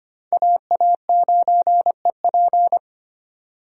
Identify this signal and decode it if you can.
Morse code, heard as a keyed tone, about 25 wpm, 700 Hz. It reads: AA9EP
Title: USA Call Signs - 5 Characters - 4x Character Spacing - 25wpm